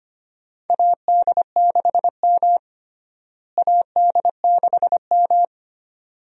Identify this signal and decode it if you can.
Morse code, heard as a keyed tone, about 25 wpm, 700 Hz. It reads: AD6M AD6M
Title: USA Call Signs - 4 Characters - Repeated 2x - 25wpm